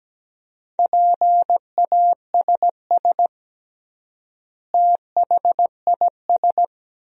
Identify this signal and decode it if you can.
Morse code, heard as a keyed tone, about 17 wpm, 700 Hz. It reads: PASS THIS
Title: 3x Word-Spacing - Sentences from Top 400 Words - 17wpm